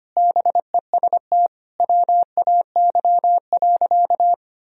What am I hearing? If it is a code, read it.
Morse code, heard as a keyed tone, about 25 wpm, 700 Hz. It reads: BEST WAY.
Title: Sentences from Top 400 Words and Abbreviations 25wpm